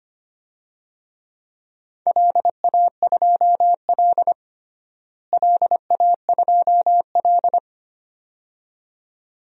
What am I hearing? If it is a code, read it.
Morse code, heard as a keyed tone, about 25 wpm, 700 Hz. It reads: LA2L LA2L
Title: DXCC-100 Call Signs - 4 Characters - Repeated 2x - 25wpm